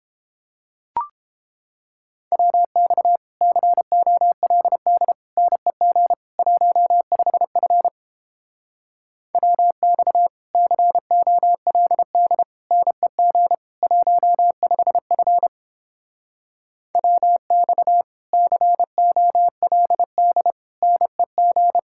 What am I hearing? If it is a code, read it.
Morse code, heard as a keyed tone, about 29 wpm, 700 Hz. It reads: WX COLD NEG 15F WX COLD NEG 15F WX COLD NEG